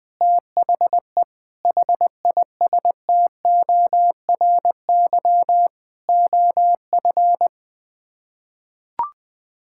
Morse code, heard as a keyed tone, about 20 wpm, 700 Hz. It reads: THE HISTORY OF
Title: Sets of 8 Words - 20wpm